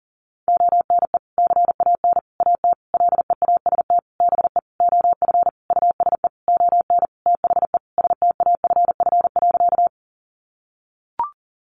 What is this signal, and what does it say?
Morse code, heard as a keyed tone, about 40 wpm, 700 Hz. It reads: ONE CAN AT LEAST BE OF USE ON THE STAFF.
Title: Sentences from Top 1200 Words 40wpm